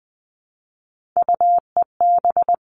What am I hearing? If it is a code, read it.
Morse code, heard as a keyed tone, about 20 wpm, 700 Hz. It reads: UEB
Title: All Cut Numbers - 3-Digit 20wpm